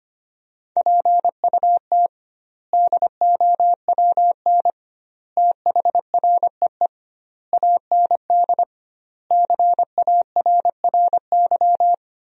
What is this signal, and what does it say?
Morse code, heard as a keyed tone, about 25 wpm, 700 Hz. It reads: PUT DOWN THREE AND CARRY
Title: 2x Word-Spacing - Sentences from Top 400 Words - 25wpm